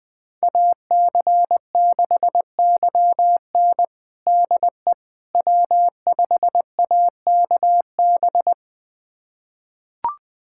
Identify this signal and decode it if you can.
Morse code, heard as a keyed tone, about 21 wpm, 700 Hz. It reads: AC6YN DE W5AKB
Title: USA Callsign DE Callsign - 5 Characters - 20wpm